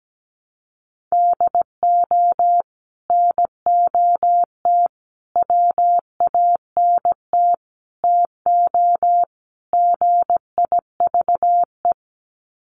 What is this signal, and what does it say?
Morse code, heard as a keyed tone, about 17 wpm, 700 Hz. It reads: DO NOT WANT TO GIVE
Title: Sets of 5 Words Encore 17wpm